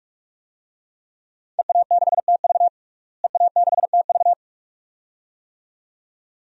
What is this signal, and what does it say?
Morse code, heard as a keyed tone, about 46 wpm, 700 Hz. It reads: EA6TV EA6TV
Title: DXCC-100 Call Signs - 5 Characters - Repeated 2x - 45wpm